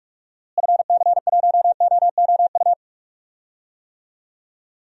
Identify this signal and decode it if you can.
Morse code, heard as a keyed tone, about 42 wpm, 700 Hz. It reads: FC1OOU
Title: DXCC-100 Call Signs - 6 Characters - Repeated 3x - 45wpm